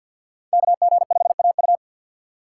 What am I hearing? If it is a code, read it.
Morse code, heard as a keyed tone, about 48 wpm, 700 Hz. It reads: KG5AU
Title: USA Call Signs - 6x Character Spacing - 50wpm